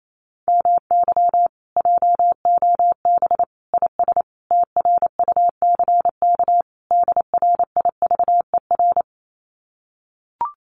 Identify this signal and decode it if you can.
Morse code, heard as a keyed tone, about 28 wpm, 700 Hz. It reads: MY JOB IS TRUCK DRIVER
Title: Common QSO Phrases Encore 28wpm